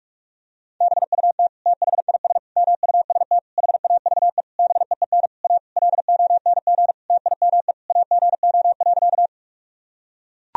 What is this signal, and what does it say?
Morse code, heard as a keyed tone, about 41 wpm, 700 Hz. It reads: BUT THIS MUST HAVE BEEN A LONG TIME AGO.
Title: Sentences from Top 1000 Words - Review Part 3 - 45wpm